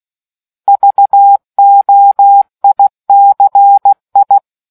Morse code, heard as a keyed tone, about 16 wpm, 800 Hz. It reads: VOICI